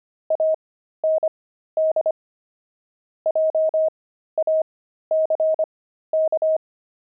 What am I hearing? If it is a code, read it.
Morse code, heard as a keyed tone, about 25 wpm, spaced as for 13 wpm, 630 Hz. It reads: AND JACK
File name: pangram-fs-13-cs-25-grumpy